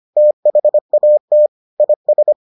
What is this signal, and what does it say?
Morse code, heard as a keyed tone, about 25 wpm, 600 Hz. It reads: THAT IS